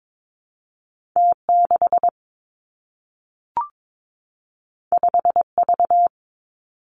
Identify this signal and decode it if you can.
Morse code, heard as a keyed tone, about 22 wpm, 700 Hz. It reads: T 6 5 V
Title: Two Challenging Letter-Number Combinations 22wpm